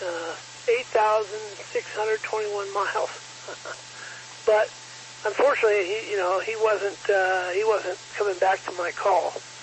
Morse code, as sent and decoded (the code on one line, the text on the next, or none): none